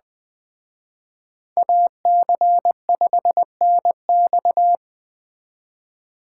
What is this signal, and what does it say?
Morse code, heard as a keyed tone, about 21 wpm, 700 Hz. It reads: AC5NX